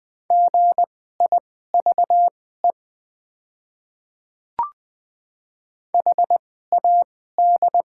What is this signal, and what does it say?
Morse code, heard as a keyed tone, about 20 wpm, spaced as for 15 wpm, 700 Hz. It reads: GIVE HAD
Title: 2x Character Spacing - Top 200 Words - 20wpm